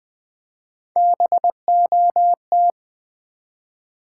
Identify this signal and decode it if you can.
Morse code, heard as a keyed tone, about 20 wpm, 700 Hz. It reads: BOT
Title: Build A Word - 3 Letters At A Time - 6-Letter Words - 20wpm